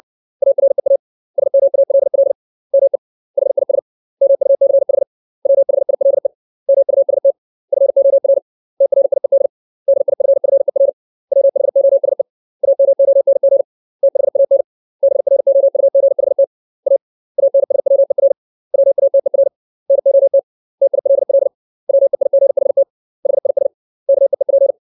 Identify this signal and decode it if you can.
Morse code, heard as a keyed tone, about 60 wpm, 550 Hz. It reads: AREA SMALL ME HIS MUCH WHILE MUST FOR TRIED DIFFER WHOSE AMONG THAN BROUGHT A ANSWER WATER NOT TILL WEIGHT HIS KEEP
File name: 632 words / 632 Words 60WPM